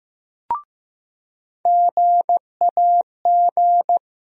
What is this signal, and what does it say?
Morse code, heard as a keyed tone, about 15 wpm, 700 Hz. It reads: GAG
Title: USA Call Sign Postfixes - 3 Letters 15wpm